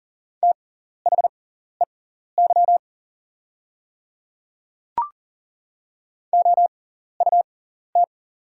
Morse code, heard as a keyed tone, about 40 wpm, spaced as for 14 wpm, 690 Hz. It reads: THEY OUT
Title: 6x Character Spacing - Top 100 Words - 40wpm